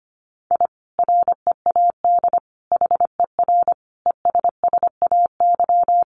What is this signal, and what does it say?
Morse code, heard as a keyed tone, about 25 wpm, 700 Hz. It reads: I READ HER ESSAY
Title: Sentences from Top 1500 Words 25wpm